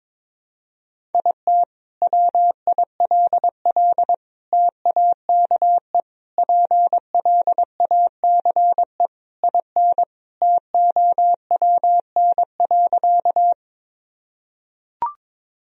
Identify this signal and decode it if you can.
Morse code, heard as a keyed tone, about 22 wpm, 700 Hz. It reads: IT WILL TAKE PLACE IN TOWN.
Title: Sentences from Top 1000 Words - Review Part 2 - 22wpm